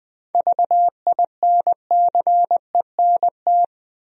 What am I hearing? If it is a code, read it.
Morse code, heard as a keyed tone, about 21 wpm, 700 Hz. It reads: VINCENT
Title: USA Common Names - 20wpm